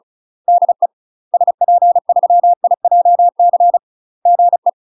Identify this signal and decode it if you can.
Morse code, heard as a keyed tone, about 34 wpm, 700 Hz. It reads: DE SP3IJC GE
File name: UK MASTER QSOs TRAINING/35WPM